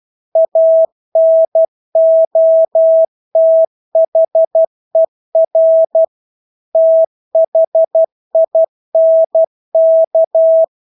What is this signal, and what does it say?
Morse code, heard as a keyed tone, about 12 wpm, 650 Hz. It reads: ANOTHER THINK
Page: Engelske ord 12 wpm